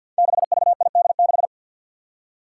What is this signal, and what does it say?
Morse code, heard as a keyed tone, about 52 wpm, 700 Hz. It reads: 6VID6
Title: Five Challenging Letter-Number Combinations 50wpm